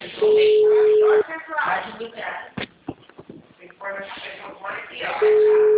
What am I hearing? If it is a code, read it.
Ringback tone (Europe, 425 Hz), with speech.
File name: out- -2002- - - .11401